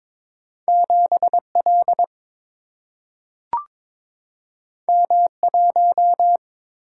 Morse code, heard as a keyed tone, about 22 wpm, 700 Hz. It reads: 7 L M 1
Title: Single-Letter Single-Number 22wpm